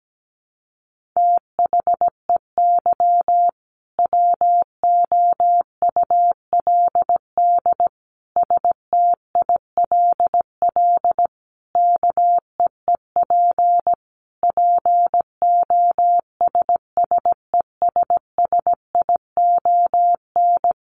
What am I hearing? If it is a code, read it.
Morse code, heard as a keyed tone, about 17 wpm, 700 Hz. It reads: THEY WOULD STILL KEEP POSSESSION
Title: Sentences from Top 1400 Words 17wpm